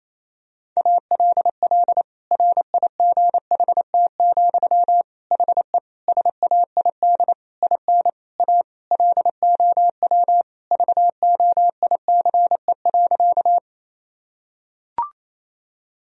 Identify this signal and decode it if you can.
Morse code, heard as a keyed tone, about 28 wpm, 700 Hz. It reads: ALL RIGHT, HE SAID IN A LOW VOICE.
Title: Sentences from Top 1000 Words - Review Part 1 - 28wpm